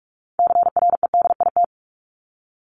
Morse code, heard as a keyed tone, about 44 wpm, 700 Hz. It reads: CREDIT
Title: Top 751-1000 Words 45wpm